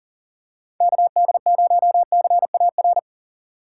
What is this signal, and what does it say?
Morse code, heard as a keyed tone, about 39 wpm, 700 Hz. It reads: KD0CAR
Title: USA Call Signs - 6x Character Spacing - 40wpm